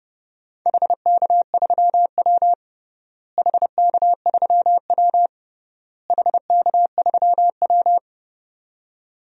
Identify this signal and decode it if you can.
Morse code, heard as a keyed tone, about 30 wpm, 700 Hz. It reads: HK3W HK3W HK3W
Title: DXCC-100 Call Signs - 4 Characters - Repeated 3x - 30wpm